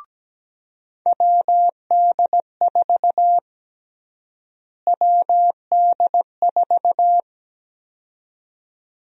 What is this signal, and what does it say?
Morse code, heard as a keyed tone, about 17 wpm, 700 Hz. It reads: WD4 WD4
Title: USA Call Sign Prefixes - Repeated 2x - 17wpm